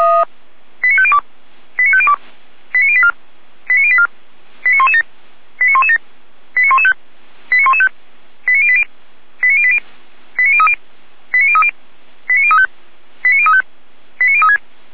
Five-tone selective call (ZVEI), heard as quick runs of five tones, tones of about 70 ms. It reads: 89502 89502 89084 89084 89117 89117 89115 89115 89080 89080 89030 89030 89035 89035 89036